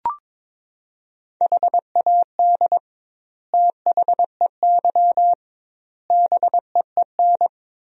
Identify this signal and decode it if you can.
Morse code, heard as a keyed tone, about 22 wpm, 700 Hz. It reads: HAD THEY BEEN
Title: 2x Word-Spacing - Sentences from Top 400 Words - 22wpm